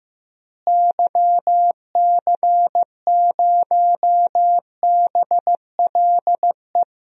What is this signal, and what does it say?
Morse code, heard as a keyed tone, about 14 wpm, 700 Hz. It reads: YC0BLE
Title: DXCC-100 Call Signs - 6 Characters - Repeated 3x - 15wpm